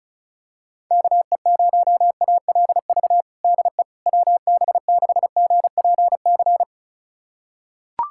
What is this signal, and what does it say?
Morse code, heard as a keyed tone, about 34 wpm, 700 Hz. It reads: KE0ALV DE WB6GPC
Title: USA Callsign DE Callsign - 6 Characters - 35wpm